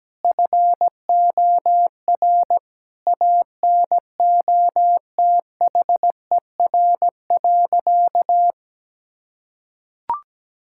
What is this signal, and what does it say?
Morse code, heard as a keyed tone, about 17 wpm, 700 Hz. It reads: FOR ANOTHER.
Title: Sentences from Top 1000 Words - Review Part 6 - 17wpm